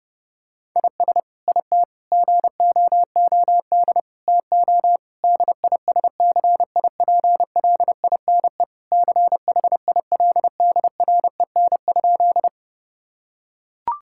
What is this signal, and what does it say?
Morse code, heard as a keyed tone, about 30 wpm, 700 Hz. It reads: IS IT GOOD TO DISCIPLINE CHILDREN?
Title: Sentences from Top 1000 Words - Review Part 6 - 30wpm